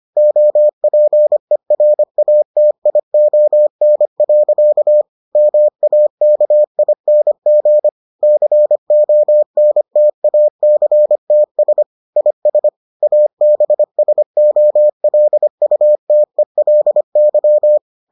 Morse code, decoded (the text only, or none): OPERATION. MAKING CONTACTS IS ABSOLUTELY